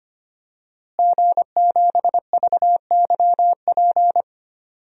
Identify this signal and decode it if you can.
Morse code, heard as a keyed tone, about 25 wpm, 700 Hz. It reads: G7VYP
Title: DXCC-100 Call Signs - 5 Characters - Repeated 2x - 25wpm